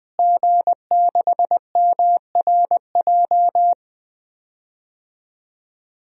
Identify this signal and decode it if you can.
Morse code, heard as a keyed tone, about 20 wpm, 700 Hz. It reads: G6MRJ
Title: DXCC-100 Call Signs - 5 Characters - Repeated 3x - 20wpm